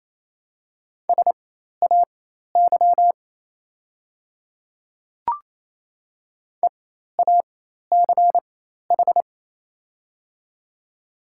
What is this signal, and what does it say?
Morse code, heard as a keyed tone, about 28 wpm, spaced as for 13 wpm, 700 Hz. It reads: SAY EACH
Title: 4x Character Spacing - Top 200 Words - 28wpm